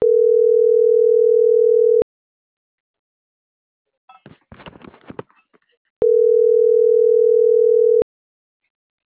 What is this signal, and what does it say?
Ringback tone (North America), 440 + 480 Hz.